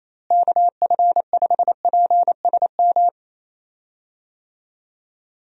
Morse code, heard as a keyed tone, about 28 wpm, 700 Hz. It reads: KF5PSM
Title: USA Call Signs - 6 Characters - Repeated 2x - 28wpm